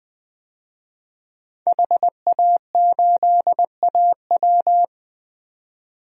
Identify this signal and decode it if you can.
Morse code, heard as a keyed tone, about 20 wpm, 700 Hz. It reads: HA8AW
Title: DXCC-100 Call Signs - 5 Characters - Repeated 3x - 20wpm